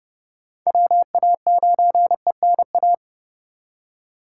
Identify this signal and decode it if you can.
Morse code, heard as a keyed tone, about 30 wpm, 700 Hz. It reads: WA9ENA
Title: K1USN SST QSO Practice - 30wpm